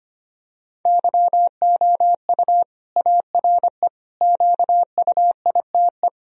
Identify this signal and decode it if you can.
Morse code, heard as a keyed tone, about 25 wpm, 700 Hz. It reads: YOU ARE QUITE
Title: Sentences from Top 1000 Words - Review Part 2 - 25wpm